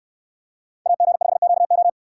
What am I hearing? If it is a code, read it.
Morse code, heard as a keyed tone, about 68 wpm, 700 Hz. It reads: AG5QO